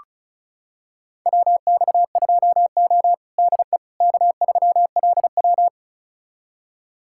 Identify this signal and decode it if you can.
Morse code, heard as a keyed tone, about 34 wpm, 700 Hz. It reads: WX2O DE K3LW
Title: USA Callsign DE Callsign - 4 Characters - 35wpm